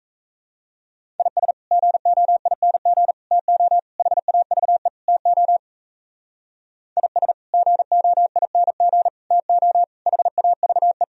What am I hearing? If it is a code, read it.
Morse code, heard as a keyed tone, about 35 wpm, 700 Hz. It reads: IS GOING TO HAVE TO IS GOING TO HAVE
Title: Speed-Racing - Sets of 5 Words 28wpm